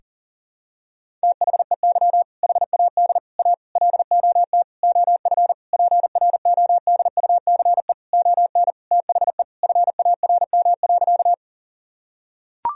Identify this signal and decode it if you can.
Morse code, heard as a keyed tone, about 37 wpm, 700 Hz. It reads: THEY HAD A LOT OF PRODUCE ON THE FARM.
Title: Sentences from Top 1000 Words - Review Part 3 - 40wpm